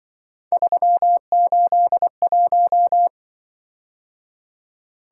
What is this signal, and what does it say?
Morse code, heard as a keyed tone, about 25 wpm, 700 Hz. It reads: 381
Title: Speed-Racing - Three Letter-Number Combinations - 20wpm